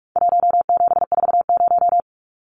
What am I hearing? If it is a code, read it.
Morse code, heard as a keyed tone, about 44 wpm, 700 Hz. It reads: 1740
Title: Numbers - 4-Digit 45wpm